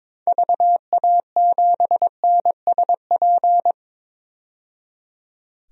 Morse code, heard as a keyed tone, about 22 wpm, 700 Hz. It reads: VA7NSP
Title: DXCC-100 Call Signs - 6 Characters - 22wpm